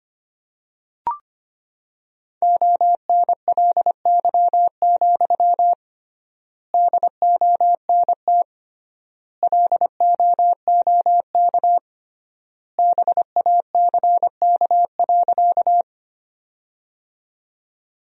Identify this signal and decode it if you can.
Morse code, heard as a keyed tone, about 25 wpm, 700 Hz. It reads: ONLY, DONT LOOK BACK.
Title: 3x Word-Spacing - Sentences from Top 300 Words - 25wpm